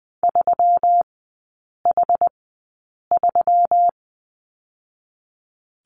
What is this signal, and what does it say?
Morse code, heard as a keyed tone, about 20 wpm, spaced as for 8 wpm, 700 Hz. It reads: 3H3
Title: 121 Lesson 20e - Number 3 - Three Characters 20wpm